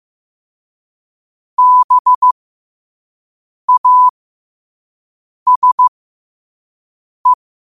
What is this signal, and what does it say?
Morse code, heard as a keyed tone, about 15 wpm, spaced as for 5 wpm, 1000 Hz. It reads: BASE